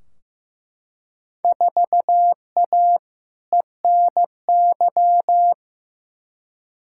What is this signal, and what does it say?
Morse code, heard as a keyed tone, about 15 wpm, 700 Hz. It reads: 4A ENY
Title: Field Day Exchanges Encore - 15wpm